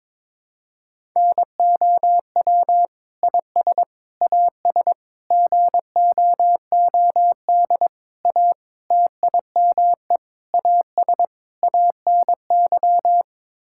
Morse code, heard as a keyed tone, about 22 wpm, 700 Hz. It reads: NOW IS AS GOOD A TIME AS ANY
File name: practice / 22wpm / Sets of 8 Words - 22wpm